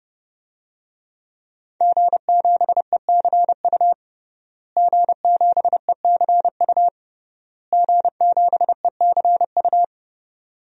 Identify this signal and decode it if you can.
Morse code, heard as a keyed tone, about 28 wpm, 700 Hz. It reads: G7ECU G7ECU G7ECU
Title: DXCC-100 Call Signs - 5 Characters - Repeated 3x - 30wpm